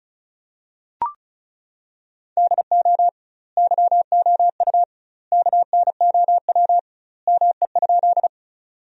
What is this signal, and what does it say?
Morse code, heard as a keyed tone, about 36 wpm, 700 Hz. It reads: DO YOU KNOW ME?
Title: 2x Word-Spacing - Sentences from Top 200 Words - 35wpm